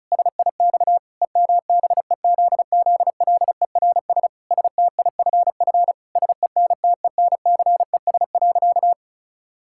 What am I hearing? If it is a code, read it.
Morse code, heard as a keyed tone, about 36 wpm, 700 Hz. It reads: SIX EMBEZZLERS STIFF SENTENCES.
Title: Pangram Challenge v2 - 35wpm